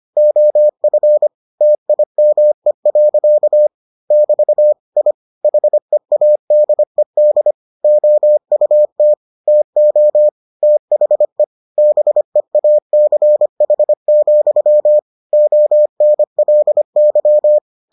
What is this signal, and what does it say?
Morse code, heard as a keyed tone, about 25 wpm, 600 Hz. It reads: OF TIME. <BT> I HEADED OUT TO THE BEACH, ONLY